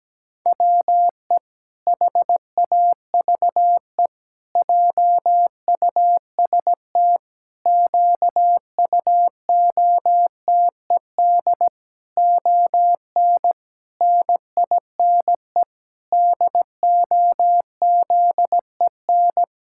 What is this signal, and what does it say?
Morse code, heard as a keyed tone, about 17 wpm, 700 Hz. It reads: WE HAVE JUST QUOTED ON NINE DOZEN
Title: Pangram Challenge v2 - 17wpm